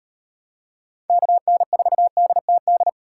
Morse code, heard as a keyed tone, about 38 wpm, 700 Hz. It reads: KN4DTD